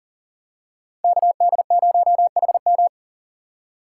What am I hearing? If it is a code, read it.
Morse code, heard as a keyed tone, about 38 wpm, 700 Hz. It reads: KD0HM
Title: USA Call Signs - 5 Characters - 2x Character Spacing - 40wpm